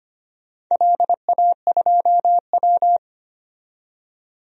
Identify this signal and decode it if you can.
Morse code, heard as a keyed tone, about 25 wpm, 700 Hz. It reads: LA2W
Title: DXCC-100 Call Signs - 4 Characters - 25wpm